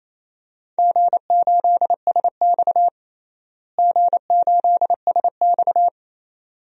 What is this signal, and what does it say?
Morse code, heard as a keyed tone, about 28 wpm, 700 Hz. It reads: G8SX G8SX
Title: DXCC-100 Call Signs - 4 Characters - Repeated 2x - 28wpm